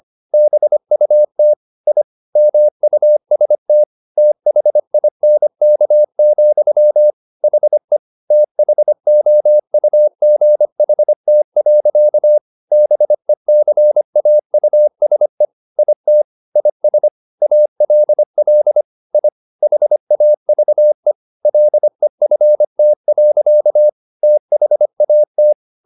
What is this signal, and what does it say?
Morse code, heard as a keyed tone, about 25 wpm, 600 Hz. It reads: BUT I MUST THINK, HE THOUGHT. BECAUSE IT IS ALL I HAVE LEFT. THAT